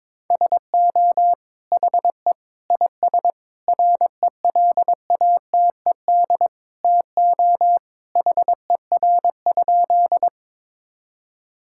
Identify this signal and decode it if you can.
Morse code, heard as a keyed tone, about 22 wpm, 700 Hz. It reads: SO HE IS RELATED TO HER?
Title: Sentences from Top 1000 Words - Review Part 5 - 22wpm